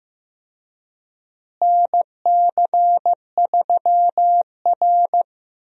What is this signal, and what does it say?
Morse code, heard as a keyed tone, about 15 wpm, 700 Hz. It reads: NC3R
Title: USA Call Signs - 4 Characters - Repeated 2x - 15wpm